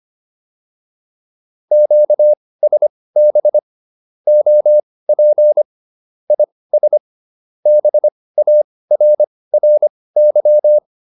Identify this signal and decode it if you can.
Morse code, heard as a keyed tone, about 25 wpm, spaced as for 18 wpm, 600 Hz. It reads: QSB OP IS BARRY